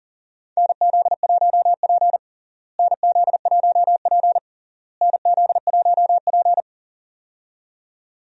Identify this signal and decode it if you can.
Morse code, heard as a keyed tone, about 40 wpm, 700 Hz. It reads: NZ1P NZ1P NZ1P
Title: USA Call Signs - 4 Characters - Repeated 3x - 40wpm